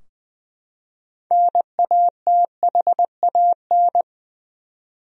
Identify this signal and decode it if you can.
Morse code, heard as a keyed tone, about 20 wpm, 700 Hz. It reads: NATHAN
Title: USA Common Names - 20wpm